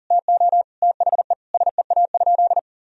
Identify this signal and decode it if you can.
Morse code, heard as a keyed tone, about 40 wpm, 700 Hz. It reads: TO THE SEA?
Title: Sentences from Top 300 Words and Abbreviations 40wpm